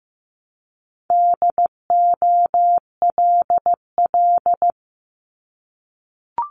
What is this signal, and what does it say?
Morse code, heard as a keyed tone, about 15 wpm, 700 Hz. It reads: DOLL
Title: Multi-letter Phonemes Repeated 3x 15wpm